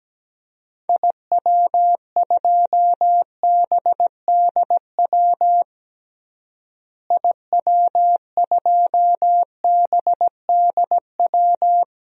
Morse code, heard as a keyed tone, about 17 wpm, 700 Hz. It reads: IW2BDW IW2BDW
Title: DXCC-100 Call Signs - 6 Characters - Repeated 3x - 17wpm